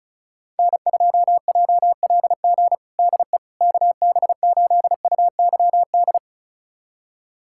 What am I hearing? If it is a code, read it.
Morse code, heard as a keyed tone, about 36 wpm, 700 Hz. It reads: N2JLG DE KB8UYD
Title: USA Callsign DE Callsign 35wpm